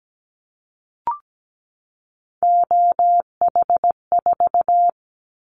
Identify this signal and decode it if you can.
Morse code, heard as a keyed tone, about 17 wpm, 700 Hz. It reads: OH4
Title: Callsign Prefixes for Top 50 DXCC - 17wpm